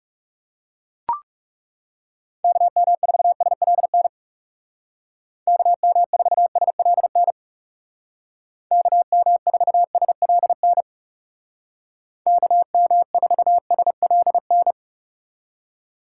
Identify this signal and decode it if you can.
Morse code, heard as a keyed tone, about 36 wpm, 700 Hz. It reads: KM4SLN KM4SLN KM4SLN KM4SLN
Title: Speed Racing - USA 2x3 Call Signs 30wpm